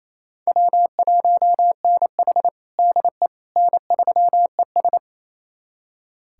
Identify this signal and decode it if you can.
Morse code, heard as a keyed tone, about 28 wpm, 700 Hz. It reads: W1NH DE N3ES
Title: USA Callsign DE Callsign - 4 Characters - 28wpm